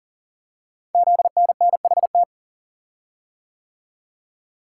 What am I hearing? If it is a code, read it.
Morse code, heard as a keyed tone, about 41 wpm, 700 Hz. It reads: ZNNHT
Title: Code Groups - 5 Letters - 40wpm